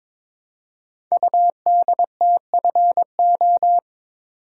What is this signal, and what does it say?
Morse code, heard as a keyed tone, about 22 wpm, 700 Hz. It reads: UDTFO
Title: Code Groups - 5 Letters - 22wpm